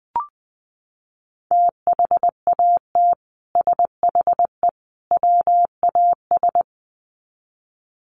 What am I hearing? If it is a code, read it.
Morse code, heard as a keyed tone, about 20 wpm, 700 Hz. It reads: THAT SHE WAS
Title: Sets of 3 Words 20wpm